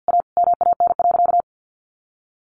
Morse code, heard as a keyed tone, about 50 wpm, 700 Hz. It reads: A MAN.
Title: Sentences from Top 300 Words 50wpm